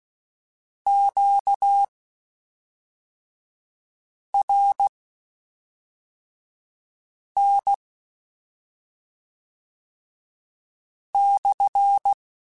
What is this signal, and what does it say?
Morse code, heard as a keyed tone, about 16 wpm, spaced as for 3 wpm, 800 Hz. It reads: QRN/
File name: Q Calls / Q Calls-mp3s / Q Calls SPSL1 @5